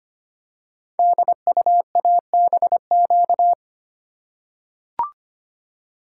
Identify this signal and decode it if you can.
Morse code, heard as a keyed tone, about 25 wpm, 700 Hz. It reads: DUABQ
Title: Code Groups - 5 Letters - 25wpm